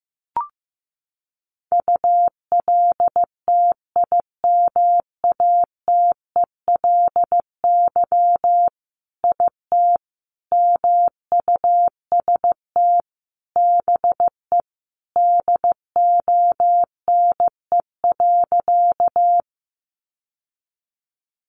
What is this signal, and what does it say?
Morse code, heard as a keyed tone, about 15 wpm, 700 Hz. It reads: ULTIMATELY IT MUST BE DONE.
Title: Sentences from Top 1200 Words 15wpm